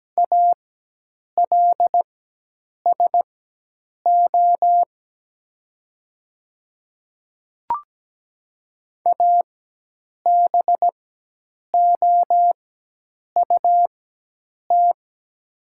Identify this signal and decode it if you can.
Morse code, heard as a keyed tone, about 17 wpm, spaced as for 8 wpm, 700 Hz. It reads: ALSO ABOUT
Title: 4x Character Spacing - Top 200 Words - 17wpm